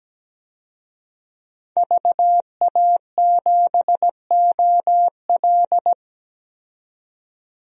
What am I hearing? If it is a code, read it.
Morse code, heard as a keyed tone, about 17 wpm, 700 Hz. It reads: VA7OL
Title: DXCC-100 Call Signs - 5 Characters - 17wpm